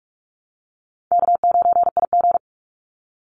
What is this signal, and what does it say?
Morse code, heard as a keyed tone, about 44 wpm, 700 Hz. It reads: K9IG
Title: Call Signs - 45wpm